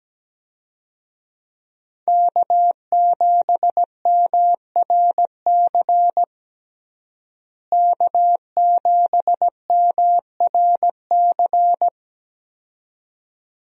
Morse code, heard as a keyed tone, about 17 wpm, 700 Hz. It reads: K7MRC K7MRC
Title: USA Call Signs - 5 Characters - Repeated 2x - 17wpm